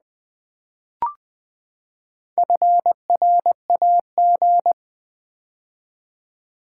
Morse code, heard as a keyed tone, about 20 wpm, 700 Hz. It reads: FRAG
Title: Build A Word - 4 Letters At A Time - 8-Letter Words - 20wpm